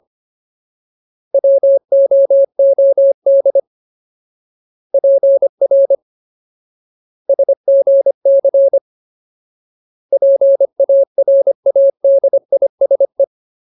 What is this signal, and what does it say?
Morse code, heard as a keyed tone, about 25 wpm, 550 Hz. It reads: WOOD PR SGC PARADISE
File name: Ultimate Word List / 25WPM Ultimate